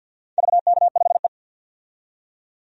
Morse code, heard as a keyed tone, about 52 wpm, 700 Hz. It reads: VK5E
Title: DXCC-100 Call Signs - 4 Characters - 50wpm